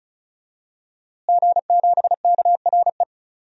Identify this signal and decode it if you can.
Morse code, heard as a keyed tone, about 34 wpm, 700 Hz. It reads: G7KRE
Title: DXCC-100 Call Signs - 5 Characters - Repeated 3x - 35wpm